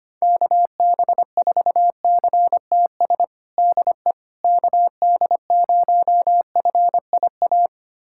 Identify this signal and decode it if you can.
Morse code, heard as a keyed tone, about 25 wpm, 700 Hz. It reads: KB4CTS DE KD0FIA
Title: USA Callsign DE Callsign 25wpm